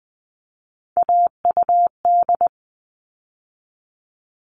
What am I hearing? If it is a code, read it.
Morse code, heard as a keyed tone, about 20 wpm, 700 Hz. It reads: AUD
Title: All Cut Numbers - 3-Digit 20wpm